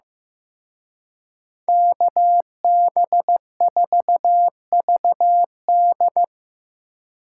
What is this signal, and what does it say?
Morse code, heard as a keyed tone, about 18 wpm, 700 Hz. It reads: KB4VD
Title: Speed Racing - USA 2x2 Call Signs 15wpm